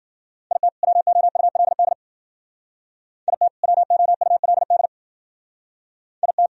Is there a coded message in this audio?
Yes: Morse code from 0.5 to 6.5 s, heard as a keyed tone, about 50 wpm, 700 Hz. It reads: IT WOULD IT WOULD IT